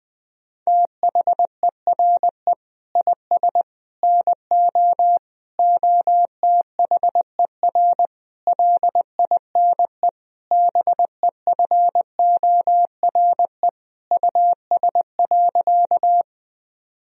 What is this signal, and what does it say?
Morse code, heard as a keyed tone, about 20 wpm, 700 Hz. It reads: THERE IS NO OTHER LINE BEFORE US.
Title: Sentences from Top 1000 Words - Review Part 1 - 20wpm